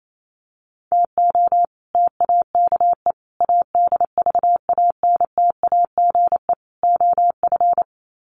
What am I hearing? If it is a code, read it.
Morse code, heard as a keyed tone, about 28 wpm, 700 Hz. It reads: TO TAKE ADVANTAGE OF
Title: Sets of 4 Words 28wpm